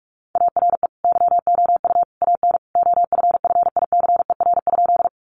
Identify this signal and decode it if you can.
Morse code, heard as a keyed tone, about 44 wpm, 700 Hz. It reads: ARE YOU AN OFFICER?
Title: Sentences from Top 1200 Words 45wpm